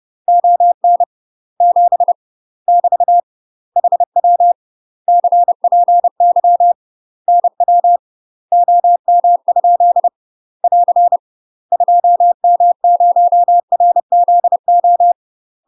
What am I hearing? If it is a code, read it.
Morse code, heard as a keyed tone, about 30 wpm, 700 Hz. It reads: ON 7 <BT> HW CPY NW OM? <AR> 2M0RZO
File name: UK MASTER QSOs TRAINING / 30WPM / UK MASTER QSOs 0002